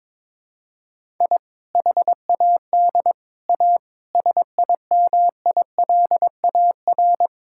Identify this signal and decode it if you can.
Morse code, heard as a keyed tone, about 22 wpm, 700 Hz. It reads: I HAD A SIMILAR
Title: Sentences from Top 1000 Words - Review Part 5 - 22wpm